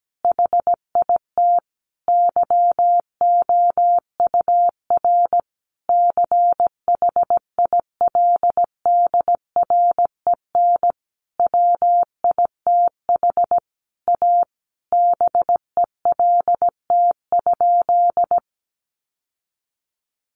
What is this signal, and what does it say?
Morse code, heard as a keyed tone, about 17 wpm, 700 Hz. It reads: HIT YOUR CHILDREN WITH A BELT?
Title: Sentences from Top 1400 Words 17wpm